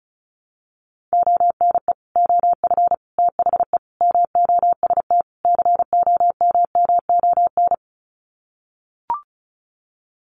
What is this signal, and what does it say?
Morse code, heard as a keyed tone, about 34 wpm, 700 Hz. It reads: ONE OF THE MOST COMMON
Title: Sets of 5 Words Encore2 35wpm